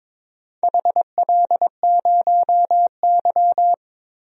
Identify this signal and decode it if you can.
Morse code, heard as a keyed tone, about 22 wpm, 700 Hz. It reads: HL0Y
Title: DXCC-100 Call Signs - 4 Characters - Repeated 2x - 22wpm